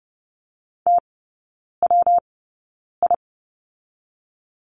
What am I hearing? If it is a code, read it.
Morse code, heard as a keyed tone, about 30 wpm, spaced as for 9 wpm, 700 Hz. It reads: TWI
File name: Lesson 19e - Letter w - Three Characters 30wpm